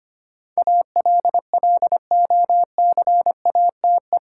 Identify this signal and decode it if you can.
Morse code, heard as a keyed tone, about 25 wpm, 700 Hz. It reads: ALLOCATE
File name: 8 Letter Words - First 3 Letters Given - 25wpm